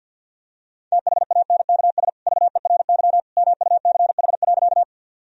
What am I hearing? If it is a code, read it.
Morse code, heard as a keyed tone, about 50 wpm, 700 Hz. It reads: THANKS VERY MUCH.
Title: Sentences from Top 1000 Words - Review Part 3 - 50wpm